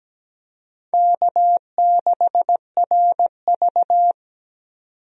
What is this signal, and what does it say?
Morse code, heard as a keyed tone, about 17 wpm, 700 Hz. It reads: K6RV
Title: USA Call Signs - 4 Characters - 17wpm